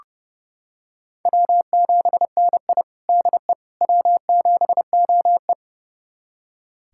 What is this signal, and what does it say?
Morse code, heard as a keyed tone, about 32 wpm, 700 Hz. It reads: W7NI DE W7OE
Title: USA Callsign DE Callsign - 4 Characters - 30wpm